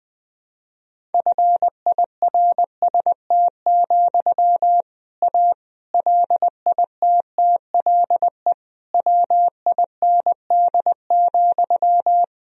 Morse code, heard as a keyed tone, about 20 wpm, 700 Hz. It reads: FIRST, A LITTLE WIND,
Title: Sentences from Top 1000 Words - Review Part 2 - 20wpm